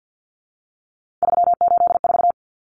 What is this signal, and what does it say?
Morse code, heard as a keyed tone, about 50 wpm, 700 Hz. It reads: 384
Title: Common Cut Numbers - 3-Digit 50wpm